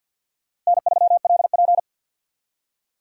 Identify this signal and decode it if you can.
Morse code, heard as a keyed tone, about 62 wpm, 700 Hz. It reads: N3LP
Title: Speed Racing - USA Call Signs 50wpm